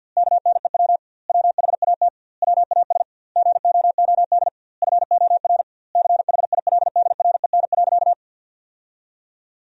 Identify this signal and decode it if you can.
Morse code, heard as a keyed tone, about 50 wpm, 700 Hz. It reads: KNEW WHAT WAS GOOD FOR CHILDREN.
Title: 2x Word-Spacing - Sentences from Top 400 Words - 50wpm